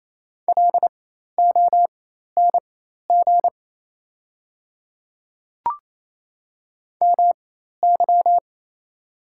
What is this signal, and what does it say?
Morse code, heard as a keyed tone, about 28 wpm, spaced as for 13 wpm, 690 Hz. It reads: LONG MY
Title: 4x Character Spacing - Top 200 Words - 28wpm